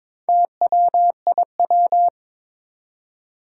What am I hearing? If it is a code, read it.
Morse code, heard as a keyed tone, about 22 wpm, 700 Hz. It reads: TWIW
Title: Four Challenging Letter-Number Combinations 22wpm